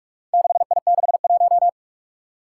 Morse code, heard as a keyed tone, about 43 wpm, 700 Hz. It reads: 6I61
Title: Four Challenging Letter-Number Combinations 45wpm